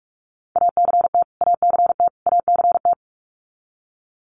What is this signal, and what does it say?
Morse code, heard as a keyed tone, about 44 wpm, 700 Hz. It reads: ACT ACT ACT
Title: Top 300 Words Repeated 45wpm